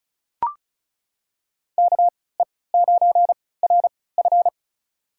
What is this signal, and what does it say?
Morse code, heard as a keyed tone, about 36 wpm, spaced as for 20 wpm, 700 Hz. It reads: KE9RF
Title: USA Call Signs - 5 Characters - 3x Character Spacing - 35wpm